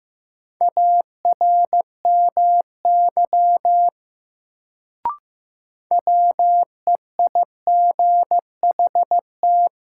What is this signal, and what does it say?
Morse code, heard as a keyed tone, about 15 wpm, 700 Hz. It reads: ARMY WEIGHT
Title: Sentences from Top 800 Words - New Word Drill - 15wpm